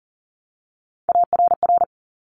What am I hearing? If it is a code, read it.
Morse code, heard as a keyed tone, about 41 wpm, 700 Hz. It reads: ARR
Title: Top Three Letter Combinations in Words 40wpm